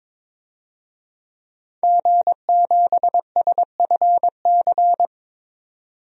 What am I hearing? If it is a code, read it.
Morse code, heard as a keyed tone, about 22 wpm, 700 Hz. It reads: G7SFC
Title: DXCC-100 Call Signs - 5 Characters - 22wpm